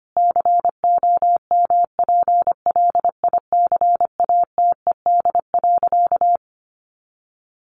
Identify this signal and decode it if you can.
Morse code, heard as a keyed tone, about 25 wpm, 700 Hz. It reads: COMPLICATED.
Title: Sentences from Top 1200 Words 25wpm